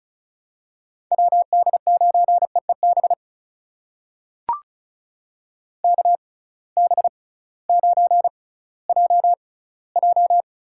Morse code, heard as a keyed tone, about 34 wpm, spaced as for 19 wpm, 700 Hz. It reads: WD9EEB KB9JJ
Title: USA Call Signs - 6 Characters - 6x Character Spacing - 35wpm